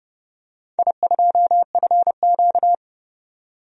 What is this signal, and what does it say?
Morse code, heard as a keyed tone, about 29 wpm, 700 Hz. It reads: I2FQ
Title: DXCC-100 Call Signs - 4 Characters - 30wpm